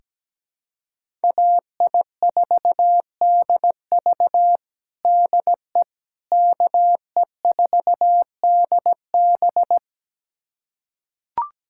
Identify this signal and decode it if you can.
Morse code, heard as a keyed tone, about 17 wpm, 700 Hz. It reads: AI4DV DE KE4DB
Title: USA Callsign DE Callsign - 5 Characters - 17wpm